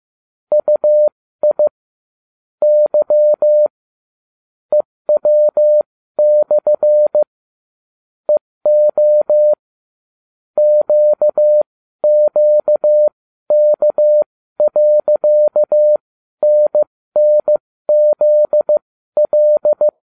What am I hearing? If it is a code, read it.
Morse code, heard as a keyed tone, about 15 wpm, spaced as for 12 wpm, 600 Hz. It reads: UI Y EW/ EO QQK.NNZL